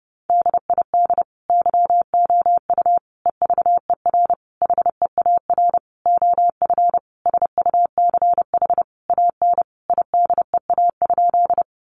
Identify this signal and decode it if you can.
Morse code, heard as a keyed tone, about 30 wpm, 700 Hz. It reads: DID YOU EVER HEAR OF SUCH AN IDEA?
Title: Sentences from Top 400 Words 30wpm